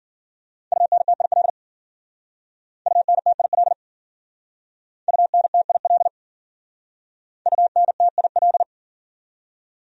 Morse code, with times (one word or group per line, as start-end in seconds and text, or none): 0.7-1.5 UNTIL
2.9-3.7 UNTIL
5.1-6.1 UNTIL
7.5-8.6 UNTIL